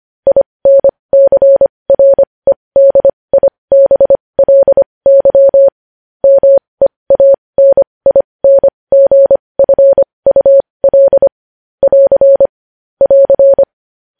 Morse code, heard as a keyed tone, about 25 wpm, spaced as for 20 wpm, 550 Hz. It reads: INCREDIBLY MEANINGFUL <AR> <AR>
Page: News Headlines in Morse Code at 20 WPM